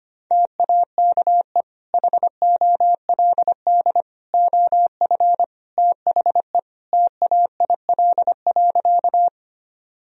Morse code, heard as a keyed tone, about 25 wpm, 700 Hz. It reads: TAKE HOLD OF THE TAIL.
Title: Sentences from Top 1000 Words - Review Part 3 - 25wpm